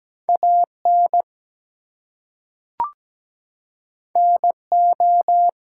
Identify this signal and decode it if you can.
Morse code, heard as a keyed tone, about 17 wpm, 700 Hz. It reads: AN NO
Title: Two Letter Words - 17wpm